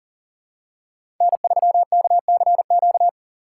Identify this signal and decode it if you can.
Morse code, heard as a keyed tone, about 40 wpm, 700 Hz. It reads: N3KCQ
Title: USA Call Signs - 5 Characters - 4x Character Spacing - 40wpm